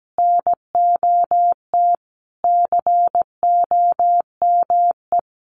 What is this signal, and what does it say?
Morse code, heard as a keyed tone, about 17 wpm, 700 Hz. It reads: NOT COME
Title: Sentences from Top 1300 Words 17wpm